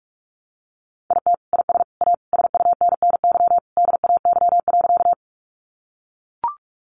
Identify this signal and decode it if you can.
Morse code, heard as a keyed tone, about 44 wpm, 700 Hz. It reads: IT IS A SUNNY DAY.
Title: Easy Phrases 45wpm